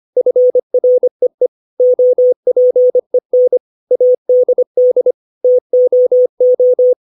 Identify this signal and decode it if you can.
Morse code, heard as a keyed tone, about 25 wpm, 500 Hz. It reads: FREE OPEN ADD TOO
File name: 632 words / 632 Words 25WPM